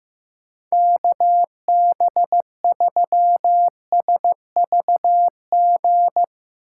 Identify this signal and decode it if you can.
Morse code, heard as a keyed tone, about 16 wpm, 700 Hz. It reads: KB3SVG
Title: Speed Racing - USA 2x3 Call Signs 15wpm